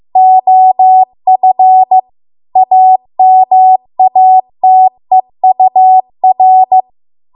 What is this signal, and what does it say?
Morse code, heard as a keyed tone, about 15 wpm, 750 Hz. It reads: OF AMATEUR